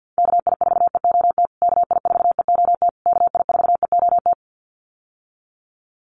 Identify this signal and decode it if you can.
Morse code, heard as a keyed tone, about 49 wpm, 700 Hz. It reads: KI4EOT KI4EOT KI4EOT
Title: USA Call Signs Repeated 3x 50wpm